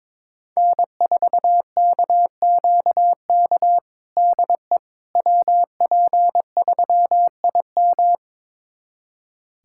Morse code, heard as a keyed tone, about 22 wpm, 700 Hz. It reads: N4KQK DE WP3IM
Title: USA Callsign DE Callsign - 5 Characters - 22wpm